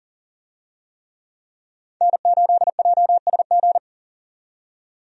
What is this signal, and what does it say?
Morse code, heard as a keyed tone, about 40 wpm, 700 Hz. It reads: N8JSG
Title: USA Call Signs - 5 Characters - 40wpm